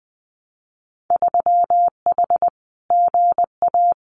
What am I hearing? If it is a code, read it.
Morse code, heard as a keyed tone, about 20 wpm, 700 Hz. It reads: 3H GA
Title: Winter Field Day Exchanges Encore 20wpm